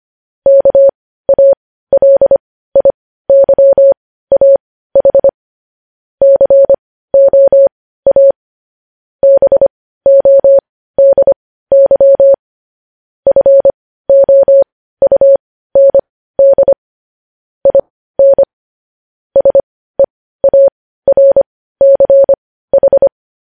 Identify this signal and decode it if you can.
Morse code, heard as a keyed tone, about 25 wpm, spaced as for 15 wpm, 550 Hz. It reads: KALIYAH COA BODY FOUND IN SEARCH
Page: news-headlines-in-morse-code-at-15-wpm